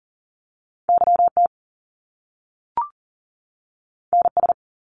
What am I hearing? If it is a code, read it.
Morse code, heard as a keyed tone, about 40 wpm, 700 Hz. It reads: YT NS
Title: Canadian Provinces and Territories 40wpm